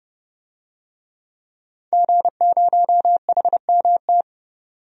Morse code, heard as a keyed tone, about 30 wpm, 700 Hz. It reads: G0HMT
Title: DXCC-100 Call Signs - 5 Characters - 30wpm